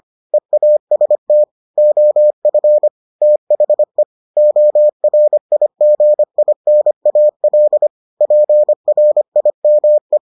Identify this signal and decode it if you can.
Morse code, heard as a keyed tone, about 25 wpm, 600 Hz. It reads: EAST OF THE ORIGINAL PRIME